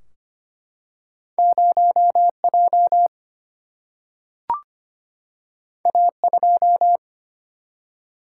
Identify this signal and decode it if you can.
Morse code, heard as a keyed tone, about 25 wpm, 700 Hz. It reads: 0J A2
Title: Single-Letter Single-Number 25wpm